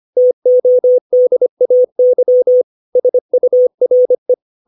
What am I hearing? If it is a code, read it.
Morse code, heard as a keyed tone, about 25 wpm, 500 Hz. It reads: TODAY SURE
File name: 632 words / 632 Words 25WPM